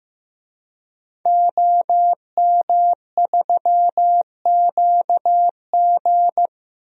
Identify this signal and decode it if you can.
Morse code, heard as a keyed tone, about 16 wpm, 700 Hz. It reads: OM3QG
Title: DXCC-100 Call Signs - 5 Characters - 15wpm